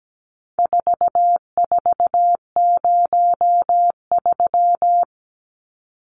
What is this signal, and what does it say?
Morse code, heard as a keyed tone, about 17 wpm, 700 Hz. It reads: 4403
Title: Numbers - 4-Digit 17wpm